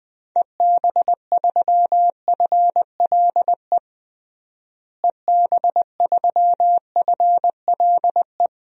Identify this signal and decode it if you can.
Morse code, heard as a keyed tone, about 20 wpm, 700 Hz. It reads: EB3FLE EB3FLE
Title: DXCC-100 Call Signs - 6 Characters - Repeated 2x - 20wpm